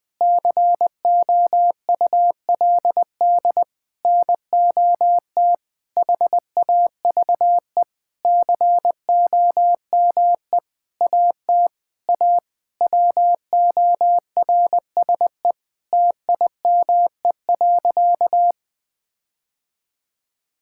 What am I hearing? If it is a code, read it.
Morse code, heard as a keyed tone, about 20 wpm, 700 Hz. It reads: COULD NOT HAVE COME AT A WORSE TIME.
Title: Sentences with 8-Word Phrases - 20wpm